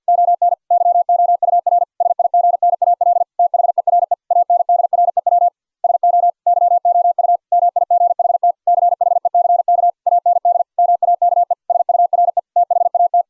Morse code, heard as a keyed tone, about 49 wpm, 690 Hz. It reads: ON YOUR SIGNAL THERE ANDREW SO YOU MIGHT CHECK AND MAKE SURE THAT